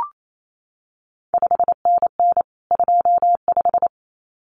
Morse code, heard as a keyed tone, about 28 wpm, 700 Hz. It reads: 5NN 25
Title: CQ WW DX Contest Exchange - 28wpm